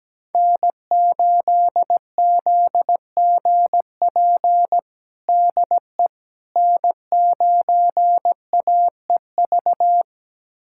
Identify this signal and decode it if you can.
Morse code, heard as a keyed tone, about 17 wpm, 700 Hz. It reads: N8ZGP DE N9AEV
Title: USA Callsign DE Callsign - 5 Characters - 17wpm